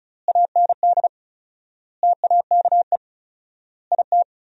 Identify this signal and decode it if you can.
Morse code, heard as a keyed tone, about 34 wpm, 700 Hz. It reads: AND TAKE IT
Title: 4x Word-Spacing - Sentences from Top 300 Words - 35wpm